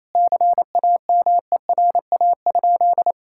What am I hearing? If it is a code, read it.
Morse code, heard as a keyed tone, about 28 wpm, 700 Hz. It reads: CAMERA?
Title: Sentences from Top 1000 Words - Review Part 5 - 28wpm